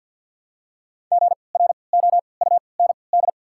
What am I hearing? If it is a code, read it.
Morse code, heard as a keyed tone, about 53 wpm, spaced as for 29 wpm, 690 Hz. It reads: GROUND
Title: 3x Character Spacing - Top 500 Words - 50wpm